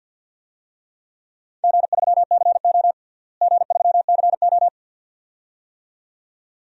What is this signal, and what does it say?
Morse code, heard as a keyed tone, about 48 wpm, 700 Hz. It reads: G3CO G3CO
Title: DXCC-100 Call Signs - 4 Characters - Repeated 2x - 50wpm